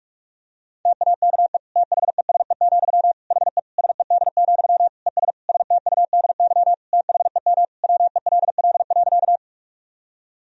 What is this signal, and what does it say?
Morse code, heard as a keyed tone, about 46 wpm, 700 Hz. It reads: TAKE THESE, HE SED, ES STUDY THEM WELL.
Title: Sentences from Top 300 Words and Abbreviations 45wpm